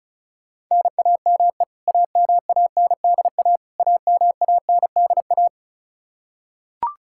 Morse code, heard as a keyed tone, about 34 wpm, 700 Hz. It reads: NAME AMANDA AMANDA
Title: K1USN SST QSO Practice 35wpm